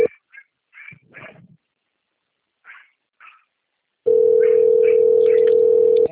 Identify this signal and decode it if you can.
Ringback tone (North America), 440 + 480 Hz.